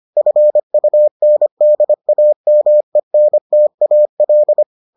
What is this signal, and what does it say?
Morse code, heard as a keyed tone, about 25 wpm, 600 Hz. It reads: FUNDAMENTAL